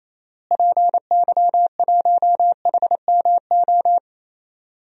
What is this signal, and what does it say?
Morse code, heard as a keyed tone, about 28 wpm, 700 Hz. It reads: PY1HMO
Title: DXCC-100 Call Signs - 6 Characters - Repeated 2x - 28wpm